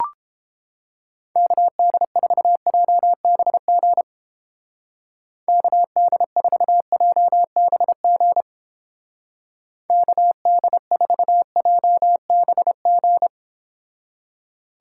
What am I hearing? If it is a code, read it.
Morse code, heard as a keyed tone, about 30 wpm, 700 Hz. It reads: KD4JBG KD4JBG KD4JBG